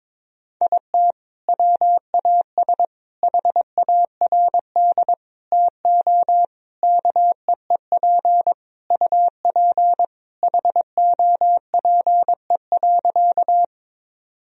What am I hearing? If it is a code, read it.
Morse code, heard as a keyed tone, about 22 wpm, 700 Hz. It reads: IT WAS HARD TO KEEP UP HOPE.
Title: Sentences from Top 1000 Words - Review Part 1 - 22wpm